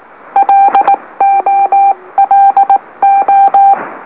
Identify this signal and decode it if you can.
Morse code, heard as a keyed tone, about 18 wpm, 790 Hz. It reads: LOLO
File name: M04- -2345utc-LOLO